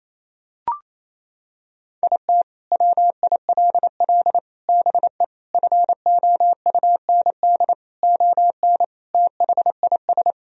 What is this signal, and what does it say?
Morse code, heard as a keyed tone, about 28 wpm, 700 Hz. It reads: IT WILL BE FOUND ON THIS
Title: Sentences from Top 1000 Words - Review Part 1 - 28wpm